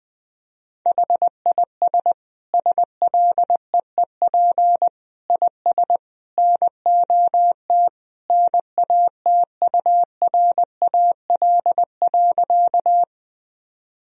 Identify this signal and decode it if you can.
Morse code, heard as a keyed tone, about 20 wpm, 700 Hz. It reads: HIS SLEEP IS NOT NATURAL.
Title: Sentences from Top 1000 Words - Review Part 4 - 20wpm